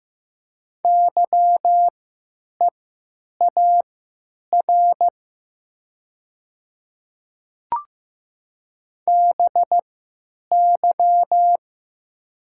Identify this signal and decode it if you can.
Morse code, heard as a keyed tone, about 15 wpm, spaced as for 9 wpm, 700 Hz. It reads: YEAR BY
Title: 3x Character Spacing - Top 300 Words - 15wpm